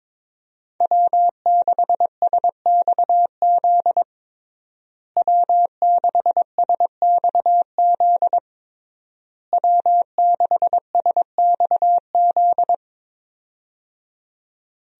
Morse code, heard as a keyed tone, about 22 wpm, 700 Hz. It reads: W6SXZ W6SXZ W6SXZ
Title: USA Call Signs - 5 Characters - Repeated 3x - 22wpm